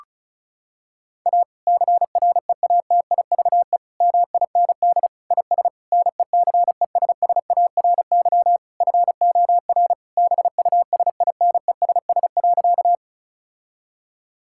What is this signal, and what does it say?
Morse code, heard as a keyed tone, about 34 wpm, 700 Hz. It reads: A CREATIVE MIND IS NECESSARY FOR BUSINESS.
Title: Sentences from Top 1000 Words - Review Part 5 - 35wpm